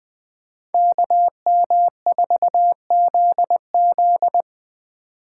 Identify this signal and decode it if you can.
Morse code, heard as a keyed tone, about 20 wpm, 700 Hz. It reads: KM4ZZ
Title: K1USN SST QSO Practice - 20wpm